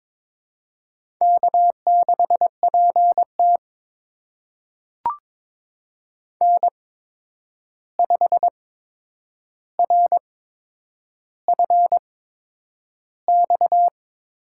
Morse code, heard as a keyed tone, about 22 wpm, spaced as for 9 wpm, 700 Hz. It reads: K6PT N5RFX
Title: USA Call Signs - 8x Character Spacing - 22wpm